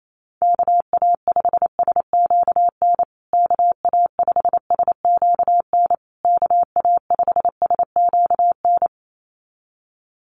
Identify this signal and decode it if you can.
Morse code, heard as a keyed tone, about 28 wpm, 700 Hz. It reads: KA5SQN KA5SQN KA5SQN
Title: USA Call Signs Repeated 3x 28wpm